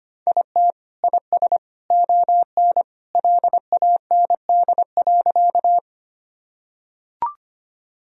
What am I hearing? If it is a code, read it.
Morse code, heard as a keyed tone, about 25 wpm, 700 Hz. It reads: IT IS ON LAND.
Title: Sentences from Top 1000 Words - Review Part 1 - 25wpm